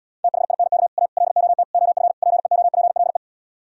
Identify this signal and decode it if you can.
Morse code, heard as a keyed tone, about 77 wpm, 700 Hz. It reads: THINK A LOT OF PEOPLE